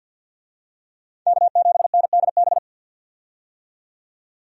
Morse code, heard as a keyed tone, about 53 wpm, 700 Hz. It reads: K7NDB
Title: USA Call Signs - 5 Characters - 50wpm